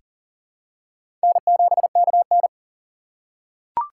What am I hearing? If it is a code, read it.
Morse code, heard as a keyed tone, about 40 wpm, 700 Hz. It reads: N7KN
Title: USA Call Signs - 4 Characters - 4x Character Spacing - 40wpm